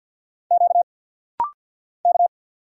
Morse code, heard as a keyed tone, about 47 wpm, 690 Hz. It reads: Q K
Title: Single Letter-Number Alt - Rapid-Fire 50wpm